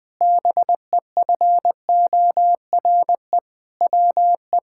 Morse code, heard as a keyed tone, about 20 wpm, 700 Hz. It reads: BEFORE WE
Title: Sentences with 7-Word Phrases - Spoken Beginning - 20wpm